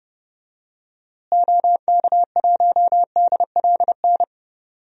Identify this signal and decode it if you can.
Morse code, heard as a keyed tone, about 30 wpm, 700 Hz. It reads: OK1DLN
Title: DXCC-100 Call Signs - 6 Characters - Repeated 3x - 30wpm